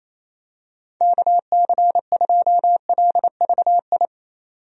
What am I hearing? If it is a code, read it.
Morse code, heard as a keyed tone, about 28 wpm, 700 Hz. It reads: KC2LVI
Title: USA Call Signs - 6 Characters - 28wpm